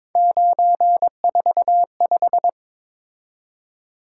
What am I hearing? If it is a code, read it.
Morse code, heard as a keyed tone, about 22 wpm, 700 Hz. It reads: 945
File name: practice / 22wpm / Three Letter-Number Combinations - 22wpm